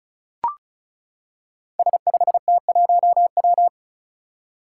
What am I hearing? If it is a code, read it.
Morse code, heard as a keyed tone, about 34 wpm, 700 Hz. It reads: S5T1W
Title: Five Challenging Letter-Number Combinations 35wpm